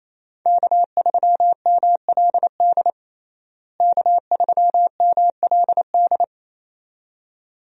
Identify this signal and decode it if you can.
Morse code, heard as a keyed tone, about 28 wpm, 700 Hz. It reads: K3MLD K3MLD
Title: USA Call Signs - 5 Characters - Repeated 2x - 28wpm